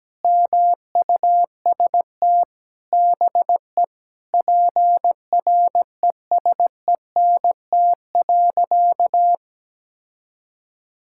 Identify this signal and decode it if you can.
Morse code, heard as a keyed tone, about 17 wpm, 700 Hz. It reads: MUST BE PRESENT.
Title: Sentences from Top 1600 Words 17wpm